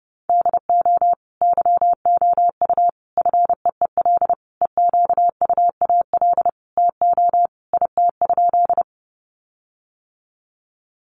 Morse code, heard as a keyed tone, about 30 wpm, 700 Hz. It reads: DO YOU FEEL EQUAL TO IT?
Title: Sentences from Top 1400 Words 30wpm